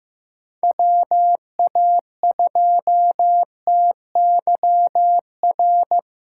Morse code, heard as a keyed tone, about 16 wpm, 700 Hz. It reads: WA2TYR
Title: USA Callsign DE Callsign - 6 Characters - 15wpm